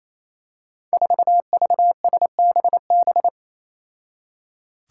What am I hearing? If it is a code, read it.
Morse code, heard as a keyed tone, about 27 wpm, 700 Hz. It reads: 4VSBB
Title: Five Challenging Letter-Number Combinations 28wpm